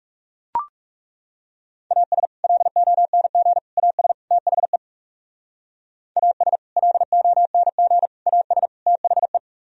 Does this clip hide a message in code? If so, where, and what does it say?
Yes: Morse code from 1.9 to 9.4 s, heard as a keyed tone, about 40 wpm, 700 Hz. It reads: AS LONG AS THE AS LONG AS THE